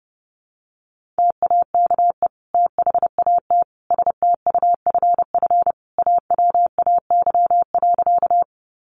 Morse code, heard as a keyed tone, about 28 wpm, 700 Hz. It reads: TAKE THAT STUFF AWAY.
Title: Sentences from Top 1400 Words 30wpm